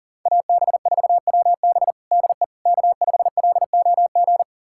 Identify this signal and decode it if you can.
Morse code, heard as a keyed tone, about 40 wpm, 700 Hz. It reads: AB4WB DE K5LOG
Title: USA Callsign DE Callsign - 5 Characters - 40wpm